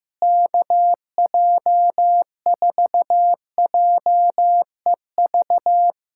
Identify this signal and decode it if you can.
Morse code, heard as a keyed tone, about 15 wpm, 700 Hz. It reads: KJ4JEV
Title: USA Callsign DE Callsign 15wpm